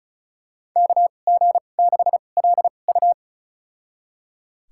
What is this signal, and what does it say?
Morse code, heard as a keyed tone, about 34 wpm, spaced as for 25 wpm, 700 Hz. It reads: KG6LU
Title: USA Call Signs - 5 Characters - 2x Character Spacing - 35wpm